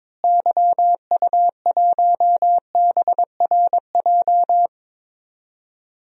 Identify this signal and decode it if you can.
Morse code, heard as a keyed tone, about 22 wpm, 700 Hz. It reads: YU1BRJ
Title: DXCC-100 Call Signs - 6 Characters - Repeated 2x - 22wpm